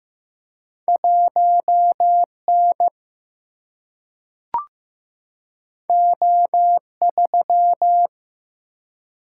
Morse code, heard as a keyed tone, about 15 wpm, 700 Hz. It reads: 1N O3
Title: Single-Letter Single-Number 15wpm